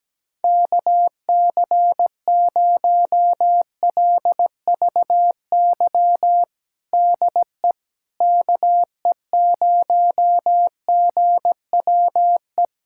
Morse code, heard as a keyed tone, about 17 wpm, 700 Hz. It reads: KC0LVY DE KE0GWE
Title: USA Callsign DE Callsign 17wpm